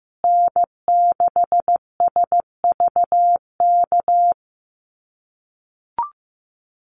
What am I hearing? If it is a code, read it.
Morse code, heard as a keyed tone, about 15 wpm, 700 Hz. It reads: N6SVK
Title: USA Call Signs Repeated 2x 15wpm